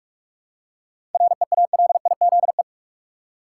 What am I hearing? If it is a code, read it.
Morse code, heard as a keyed tone, about 42 wpm, 700 Hz. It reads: REALIZE
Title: Seven Letter Words - 45wpm